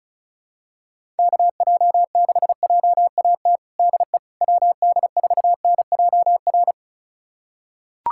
Morse code, heard as a keyed tone, about 34 wpm, 700 Hz. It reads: KJ6JAT DE WD4NJR
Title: USA Callsign DE Callsign - 6 Characters - 35wpm